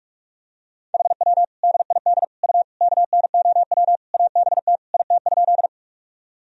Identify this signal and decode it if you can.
Morse code, heard as a keyed tone, about 44 wpm, 700 Hz. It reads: HW DID U KNOW ABT IT?
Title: Sentences from Top 100 Words and Abbreviations 45wpm